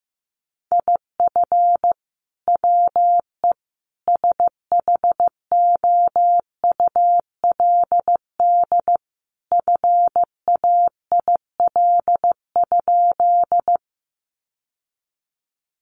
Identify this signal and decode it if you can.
Morse code, heard as a keyed tone, about 15 wpm, 700 Hz. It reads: IF WE SHOULD FAIL?
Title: Sentences from Top 1100 Words 15wpm